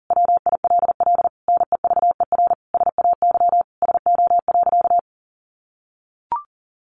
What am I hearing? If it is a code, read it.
Morse code, heard as a keyed tone, about 40 wpm, 700 Hz. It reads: WILL NEVER SAY SO.
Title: Sentences from Top 1400 Words 40wpm